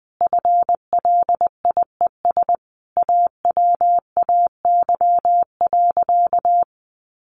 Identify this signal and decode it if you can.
Morse code, heard as a keyed tone, about 19 wpm, 700 Hz. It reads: FLIES AWAY.
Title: Additional Easy Phrases 20wpm